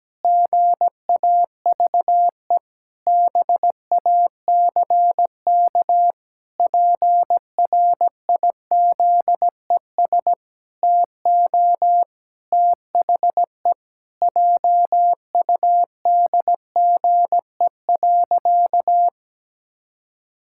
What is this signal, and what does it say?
Morse code, heard as a keyed tone, about 17 wpm, 700 Hz. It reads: GAVE BACK PRIZES TO THE JUDGE.
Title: Pangram Challenge v2 - 17wpm